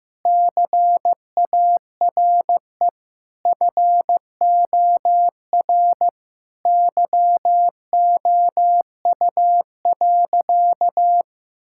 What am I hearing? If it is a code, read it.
Morse code, heard as a keyed tone, about 16 wpm, 700 Hz. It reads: CARE FOR YOU.
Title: Sentences from Top 1000 Words - Review Part 6 - 15wpm